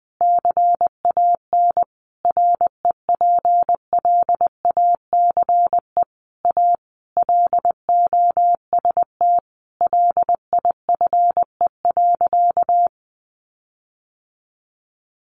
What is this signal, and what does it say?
Morse code, heard as a keyed tone, about 20 wpm, 700 Hz. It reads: CAN REPLACE A LOST LIFE.
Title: Sentences from Top 1400 Words 20wpm